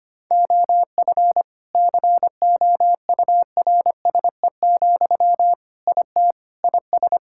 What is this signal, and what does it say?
Morse code, heard as a keyed tone, about 25 wpm, 700 Hz. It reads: OF COURSE, IT IS
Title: Sentences from Top 1600 Words 25wpm